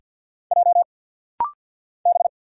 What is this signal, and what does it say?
Morse code, heard as a keyed tone, about 50 wpm, 700 Hz. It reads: J B
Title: Single Letters - Rapid-Fire 50wpm